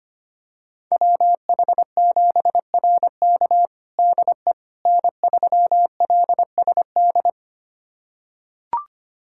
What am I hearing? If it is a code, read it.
Morse code, heard as a keyed tone, about 25 wpm, 700 Hz. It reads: WH7RK DE N3LSD
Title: USA Callsign DE Callsign - 5 Characters - 25wpm